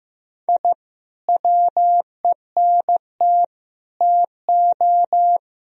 Morse code, heard as a keyed tone, about 16 wpm, 700 Hz. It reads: I WENT TO
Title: Sentences from Top 1000 Words - Review Part 2 - 15wpm